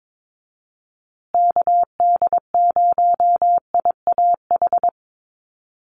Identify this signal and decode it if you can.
Morse code, heard as a keyed tone, about 22 wpm, 700 Hz. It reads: KD0IAH
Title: USA Call Signs Repeated 2x 22wpm